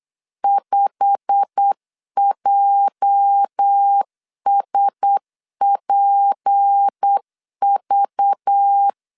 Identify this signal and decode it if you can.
Morse code, heard as a keyed tone, about 8 wpm, 800 Hz. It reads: 5JSPV
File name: CW / DET / 1 9 1500C S A 8wpm 5637